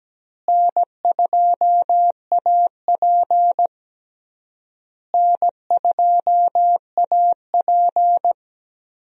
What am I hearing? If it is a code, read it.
Morse code, heard as a keyed tone, about 17 wpm, 700 Hz. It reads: N2AP N2AP
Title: USA Call Signs - 4 Characters - Repeated 3x - 17wpm